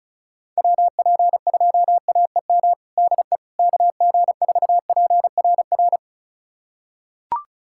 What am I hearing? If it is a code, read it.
Morse code, heard as a keyed tone, about 36 wpm, 700 Hz. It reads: WP2AEM DE KG4PRR
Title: USA Callsign DE Callsign - 6 Characters - 35wpm